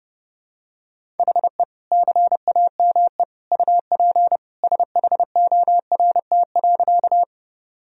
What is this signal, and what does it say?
Morse code, heard as a keyed tone, about 32 wpm, 700 Hz. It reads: HE CAME UP SHORT.
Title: Sentences from Top 1000 Words - Review Part 2 - 30wpm